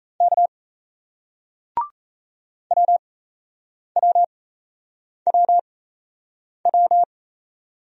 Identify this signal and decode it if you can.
Morse code, heard as a keyed tone, about 35 wpm, spaced as for 8 wpm, 700 Hz. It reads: K WWWW